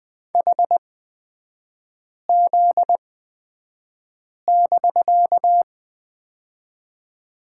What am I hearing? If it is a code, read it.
Morse code, heard as a keyed tone, about 20 wpm, spaced as for 5 wpm, 700 Hz. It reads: HZ<BK>
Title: 250 Lesson 40e - BK - Three Characters 20wpm